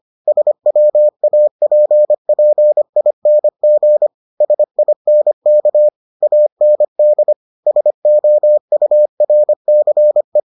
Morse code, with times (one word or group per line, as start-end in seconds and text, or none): 0.3-4.1 SWAPPING
4.4-5.9 SINK
6.2-7.3 AND
7.7-10.4 SOURCE